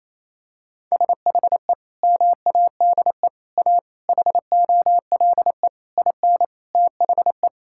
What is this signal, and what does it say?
Morse code, heard as a keyed tone, about 28 wpm, 700 Hz. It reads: SHE MADE A HOLE IN THE
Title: Sentences from Top 1600 Words 28wpm